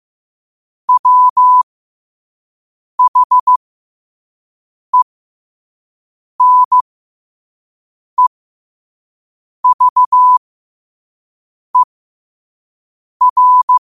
Morse code, heard as a keyed tone, about 15 wpm, spaced as for 5 wpm, 1000 Hz. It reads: WHENEVER
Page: New quotes every day in morse code at 5 Words per minute.